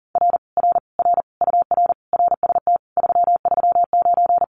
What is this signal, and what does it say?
Morse code, heard as a keyed tone, about 38 wpm, 700 Hz. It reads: R R R UR RST 339
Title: Common QSO Phrases Encore 40wpm